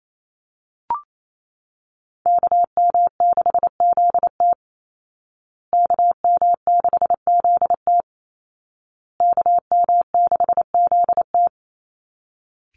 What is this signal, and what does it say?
Morse code, heard as a keyed tone, about 28 wpm, 700 Hz. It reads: KM6ZT KM6ZT KM6ZT
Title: Call Signs Repeated 2x - 28wpm